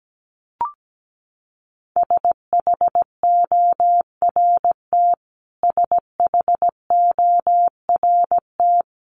Morse code, heard as a keyed tone, about 17 wpm, 700 Hz. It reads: SHORT SHORT
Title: Top 500 Words Repeated 17wpm